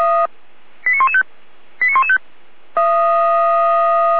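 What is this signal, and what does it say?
Five-tone selective call (ZVEI), heard as quick runs of five tones, tones of about 70 ms. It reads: 89115 89115
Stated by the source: ZVEI five-tone selective call received on an SDR.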